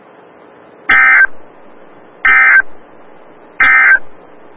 Emergency Alert System (SAME) tones, heard as harsh buzzing bursts. It is heard from 0.8 to 4.1 s.
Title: Des Moines, IA NOAA Weather Radio - WXL57